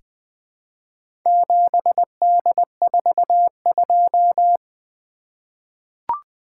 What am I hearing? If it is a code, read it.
Morse code, heard as a keyed tone, about 20 wpm, 700 Hz. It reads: 7D42
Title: Four Letter-Number Combinations - 20wpm